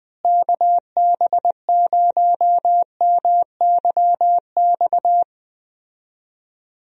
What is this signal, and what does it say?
Morse code, heard as a keyed tone, about 20 wpm, 700 Hz. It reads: KB0MYX
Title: USA Call Signs - 6 Characters - Repeated 3x - 20wpm